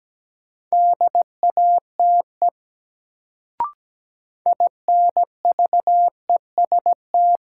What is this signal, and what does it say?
Morse code, heard as a keyed tone, about 17 wpm, 700 Hz. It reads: DATE INVEST
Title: Sentences from Top 1100 Words - New Word Drill Alt - 17wpm